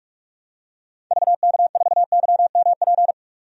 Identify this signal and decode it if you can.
Morse code, heard as a keyed tone, about 45 wpm, 700 Hz. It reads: VK4YMP
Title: DXCC-100 Call Signs - 6 Characters - 45wpm